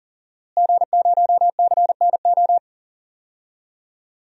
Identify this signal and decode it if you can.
Morse code, heard as a keyed tone, about 40 wpm, 700 Hz. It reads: G0CNO
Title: DXCC-100 Call Signs - 5 Characters - Repeated 3x - 40wpm